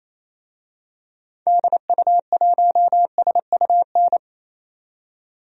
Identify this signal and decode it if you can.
Morse code, heard as a keyed tone, about 28 wpm, 700 Hz. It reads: DU1SUN
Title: DXCC-100 Call Signs - 6 Characters - 28wpm